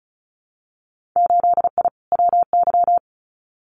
Morse code, heard as a keyed tone, about 36 wpm, 700 Hz. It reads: 8I WY
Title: Winter Field Day Exchanges Encore 35wpm